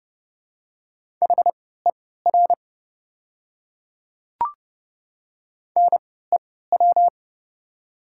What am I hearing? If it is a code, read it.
Morse code, heard as a keyed tone, about 30 wpm, spaced as for 17 wpm, 700 Hz. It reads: HER NEW
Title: 3x Character Spacing - Top 200 Words - 30wpm